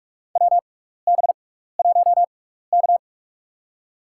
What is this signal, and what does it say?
Morse code, heard as a keyed tone, about 46 wpm, spaced as for 16 wpm, 700 Hz. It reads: WB1K
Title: USA Call Signs - 4 Characters - 6x Character Spacing - 45wpm